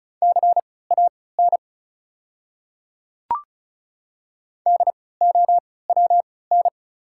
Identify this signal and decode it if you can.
Morse code, heard as a keyed tone, about 34 wpm, spaced as for 20 wpm, 700 Hz. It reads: CAN DOWN
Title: 3x Character Spacing - Top 500 Words - 35wpm